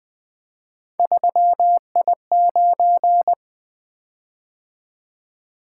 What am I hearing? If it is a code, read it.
Morse code, heard as a keyed tone, about 20 wpm, 700 Hz. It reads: 3I9
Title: Three Letter-Number Combinations - 20wpm